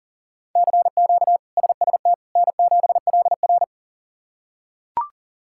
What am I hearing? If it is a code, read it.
Morse code, heard as a keyed tone, about 40 wpm, 700 Hz. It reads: CQ SST N7LR
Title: K1USN SST QSO Practice - 40wpm